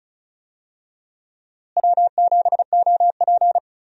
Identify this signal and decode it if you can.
Morse code, heard as a keyed tone, about 36 wpm, 700 Hz. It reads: W7OP
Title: USA Call Signs - 4 Characters - Repeated 2x - 35wpm